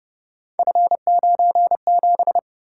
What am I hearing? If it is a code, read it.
Morse code, heard as a keyed tone, about 31 wpm, 700 Hz. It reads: F97
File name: Speed-Racing - Three Letter-Number Combinations - 25wpm